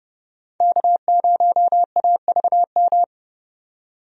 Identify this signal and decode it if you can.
Morse code, heard as a keyed tone, about 30 wpm, 700 Hz. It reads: K0AVM
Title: USA Call Signs - 5 Characters - 30wpm